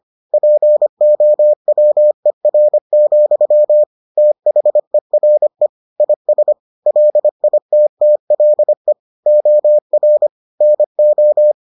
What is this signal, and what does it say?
Morse code, heard as a keyed tone, about 25 wpm, 600 Hz. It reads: POWER, THERE IS LITTLE OR NO